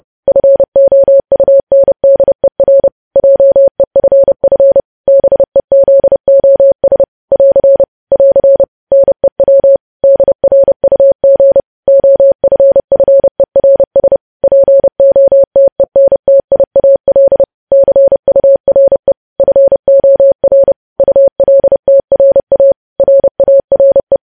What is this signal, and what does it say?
Morse code, heard as a keyed tone, about 30 wpm, 550 Hz. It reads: FOUNDER JEFF BEZOS <AR> <AR> NEW DRUG OFFERS POTENTIAL CURE FOR ULTRA RARE
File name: podcast / News headlines-1x-30x30